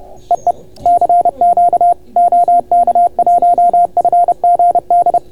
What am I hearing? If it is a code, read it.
Morse code, heard as a keyed tone, about 31 wpm, 690 Hz. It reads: EE CQ OK1FGD